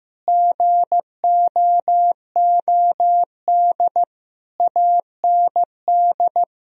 Morse code, heard as a keyed tone, about 16 wpm, 700 Hz. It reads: GOOD AND
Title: Sentences from Top 1000 Words - Review Part 2 - 15wpm